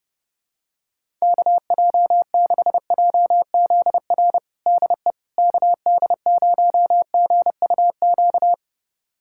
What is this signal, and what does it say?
Morse code, heard as a keyed tone, about 30 wpm, 700 Hz. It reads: KJ6JZR DE KD0GUQ
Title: USA Callsign DE Callsign 30wpm